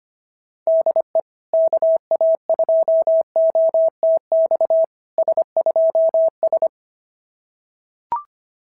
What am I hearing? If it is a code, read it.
Morse code, heard as a keyed tone, about 27 wpm, 650 Hz. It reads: DE KA2OTX S2S
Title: SOTA QSO Practice 25wpm